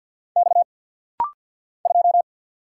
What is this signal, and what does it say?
Morse code, heard as a keyed tone, about 50 wpm, 690 Hz. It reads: X 2
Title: Single Letter-Number Alt - Rapid-Fire 50wpm